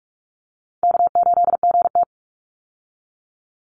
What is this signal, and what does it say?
Morse code, heard as a keyed tone, about 44 wpm, 700 Hz. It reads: K8GT
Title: Call Signs - 45wpm